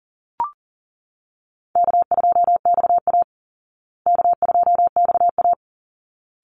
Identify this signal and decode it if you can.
Morse code, heard as a keyed tone, about 40 wpm, 700 Hz. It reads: K2XA K2XA
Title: Easy Call Signs Repeated 2x - 40wpm